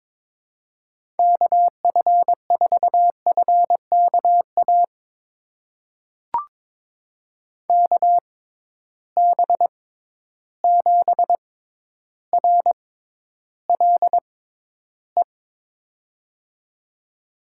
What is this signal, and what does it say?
Morse code, heard as a keyed tone, about 22 wpm, spaced as for 11 wpm, 700 Hz. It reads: KF4FKA KB7RLE
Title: USA Call Signs - 6 Characters - 6x Character Spacing - 22wpm